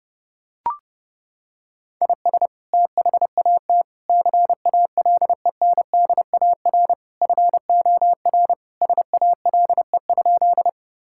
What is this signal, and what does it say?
Morse code, heard as a keyed tone, about 30 wpm, 700 Hz. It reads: IS THAT CALENDAR FOR SALE?
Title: Sentences from Top 1600 Words 30wpm